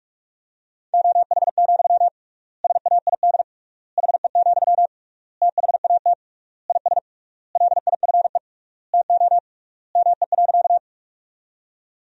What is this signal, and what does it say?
Morse code, heard as a keyed tone, about 41 wpm, 700 Hz. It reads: OH, SAID HE, THAT IS LIFE TO ME.
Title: 3x Word-Spacing - Sentences from Top 500 Words - 45wpm